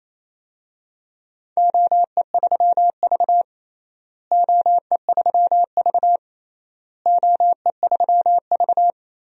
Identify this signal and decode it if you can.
Morse code, heard as a keyed tone, about 27 wpm, 700 Hz. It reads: OE3V OE3V OE3V
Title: DXCC-100 Call Signs - 4 Characters - Repeated 3x - 28wpm